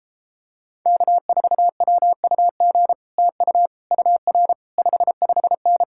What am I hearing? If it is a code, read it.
Morse code, heard as a keyed tone, about 33 wpm, 700 Hz. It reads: K4WUG TU UR 55N